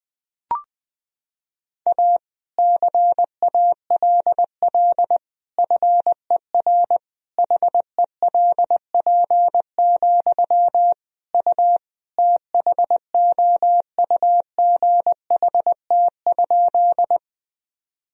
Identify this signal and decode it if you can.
Morse code, heard as a keyed tone, about 21 wpm, 700 Hz. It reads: A CALL FER HELP, U THOUGHT?
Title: Sentences from Top 300 Words and Abbreviations 20wpm